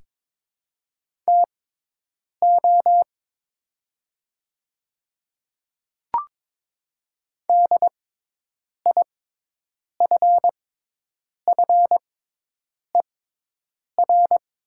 Morse code, heard as a keyed tone, about 22 wpm, spaced as for 8 wpm, 700 Hz. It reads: TO DIFFER
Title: 6x Character Spacing - Top 200 Words - 22wpm